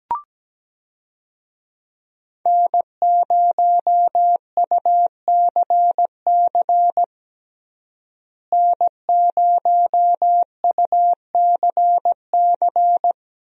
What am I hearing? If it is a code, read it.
Morse code, heard as a keyed tone, about 17 wpm, 700 Hz. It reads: N0UCC N0UCC
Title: USA Call Signs - 5 Characters - Repeated 2x - 17wpm